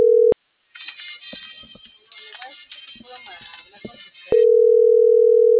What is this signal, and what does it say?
Ringback tone (North America), 440 + 480 Hz.